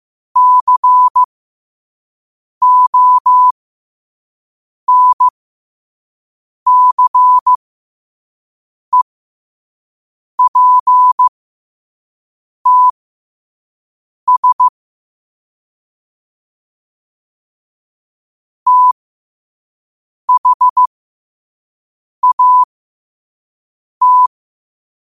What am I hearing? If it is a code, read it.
Morse code, heard as a keyed tone, about 15 wpm, spaced as for 5 wpm, 1000 Hz. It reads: CONCEPTS THAT
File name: qotdfiles / 5wpm